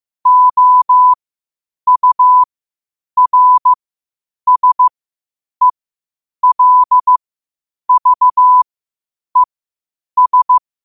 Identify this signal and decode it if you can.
Morse code, heard as a keyed tone, about 15 wpm, spaced as for 8 wpm, 1000 Hz. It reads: OURSELVES
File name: qotdfiles / 8wpm